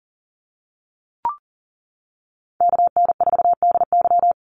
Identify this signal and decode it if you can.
Morse code, heard as a keyed tone, about 40 wpm, 700 Hz. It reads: KN4DY
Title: USA Call Signs 40wpm